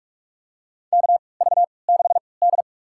Morse code, heard as a keyed tone, about 46 wpm, spaced as for 25 wpm, 700 Hz. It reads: KV6D
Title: USA Call Signs - 3x Character Spacing - 45wpm